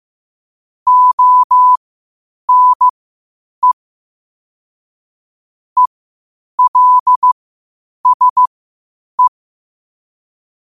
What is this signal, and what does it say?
Morse code, heard as a keyed tone, about 15 wpm, spaced as for 8 wpm, 1000 Hz. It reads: ONE ELSE
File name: qotdfiles / 8wpm